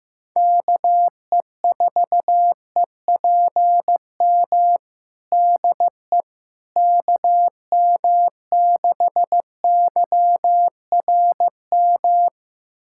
Morse code, heard as a keyed tone, about 15 wpm, 700 Hz. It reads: KE4EPM DE KM6YRM
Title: USA Callsign DE Callsign - 6 Characters - 15wpm